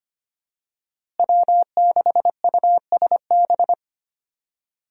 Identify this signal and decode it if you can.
Morse code, heard as a keyed tone, about 25 wpm, 700 Hz. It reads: W6USB
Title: USA Call Signs - 5 Characters - Repeated 2x - 25wpm